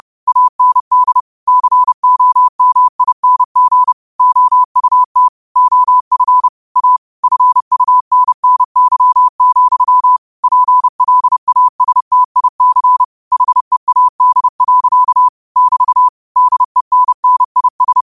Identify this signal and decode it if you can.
Morse code, heard as a keyed tone, about 30 wpm, 1000 Hz. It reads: AND COMING OUT OF A FUNNY, PLASTIC HEAD. <BT> DENNIS